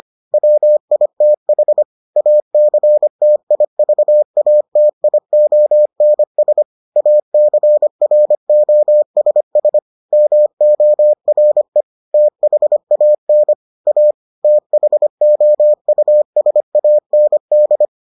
Morse code, heard as a keyed tone, about 25 wpm, 600 Hz. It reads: WITH ACTIVATIONS ACROSS MORE THAN A THOUSAND